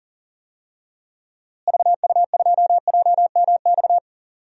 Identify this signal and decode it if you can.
Morse code, heard as a keyed tone, about 38 wpm, 700 Hz. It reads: VU2JMX
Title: DXCC-100 Call Signs - 6 Characters - Repeated 3x - 40wpm